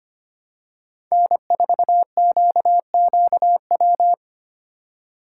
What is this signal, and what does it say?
Morse code, heard as a keyed tone, about 25 wpm, 700 Hz. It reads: N4QQW
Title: USA Call Signs - 5 Characters - 4x Character Spacing - 25wpm